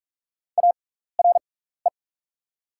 Morse code, heard as a keyed tone, about 44 wpm, spaced as for 15 wpm, 690 Hz. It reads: ARE
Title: 6x Character Spacing - Top 200 Words - 45wpm